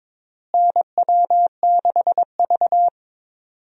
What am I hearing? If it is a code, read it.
Morse code, heard as a keyed tone, about 22 wpm, 700 Hz. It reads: NW6V
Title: USA Call Signs - 4 Characters - Repeated 3x - 22wpm